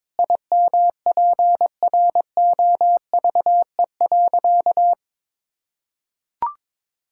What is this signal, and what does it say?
Morse code, heard as a keyed tone, about 22 wpm, 700 Hz. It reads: IMPROVE.
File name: practice / 22wpm / Sentences with 4-Word Phrases - 22wpm